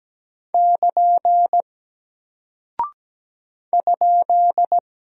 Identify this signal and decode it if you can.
Morse code, heard as a keyed tone, about 17 wpm, 700 Hz. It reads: <KN> ?
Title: Single Character (Letter:Number:Punctuation:Prosign) Alt 17wpm